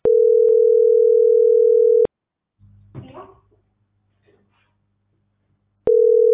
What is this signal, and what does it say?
Ringback tone (North America), 440 + 480 Hz.